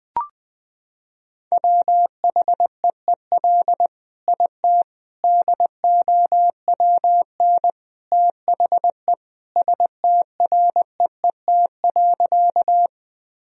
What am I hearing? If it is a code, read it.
Morse code, heard as a keyed tone, about 20 wpm, 700 Hz. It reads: WHEEL IT DOWN THE STREET.
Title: Sentences from Top 1000 Words - Review Part 3 - 20wpm